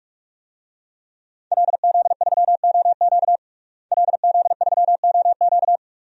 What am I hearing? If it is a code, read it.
Morse code, heard as a keyed tone, about 41 wpm, 700 Hz. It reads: LZ3OQ LZ3OQ
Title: DXCC-100 Call Signs - 5 Characters - Repeated 3x - 45wpm